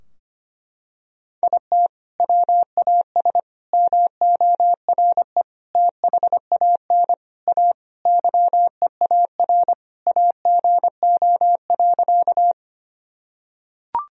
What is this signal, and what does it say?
Morse code, heard as a keyed tone, about 25 wpm, 700 Hz. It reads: IT WAS MORE THAN A YEAR AGO.
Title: Sentences from Top 1000 Words - Review Part 3 - 25wpm